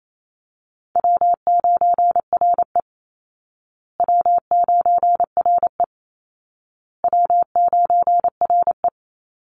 Morse code, heard as a keyed tone, about 28 wpm, 700 Hz. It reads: W9RE W9RE W9RE
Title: Call Signs Repeated 2x - 28wpm